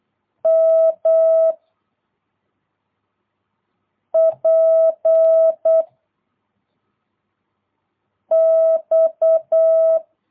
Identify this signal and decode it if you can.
Morse code, heard as a keyed tone, about 8 wpm, 660 Hz. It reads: M P X
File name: jh8cbh / 2022cw